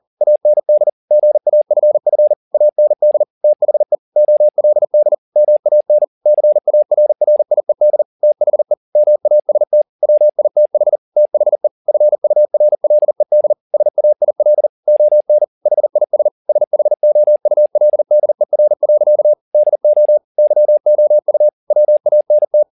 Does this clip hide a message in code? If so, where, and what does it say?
Yes: Morse code from 0.2 to 22.6 s, heard as a keyed tone, about 40 wpm, 600 Hz. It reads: AND GAFF AND THE OLD MAN CARRIED THE MAST WITH THE FURLED SAIL ON HIS SHOULDER. DO YOU WANT